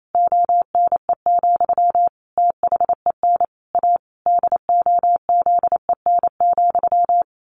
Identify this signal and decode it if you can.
Morse code, heard as a keyed tone, about 28 wpm, 700 Hz. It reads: ONE, THEN A DOZEN,
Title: A Sound Of Thunder 28wpm